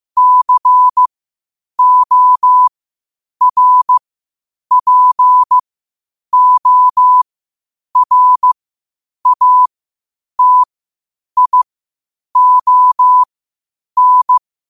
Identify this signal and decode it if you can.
Morse code, heard as a keyed tone, about 15 wpm, spaced as for 8 wpm, 1000 Hz. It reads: CORPORATION
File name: qotdfiles / 8wpm